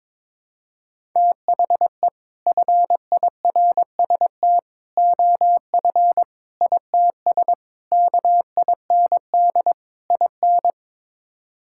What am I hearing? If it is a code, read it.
Morse code, heard as a keyed tone, about 21 wpm, 700 Hz. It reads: THE FIRST OF ITS KIND IN
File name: practice / 22wpm / Sets of 6 Words - 22wpm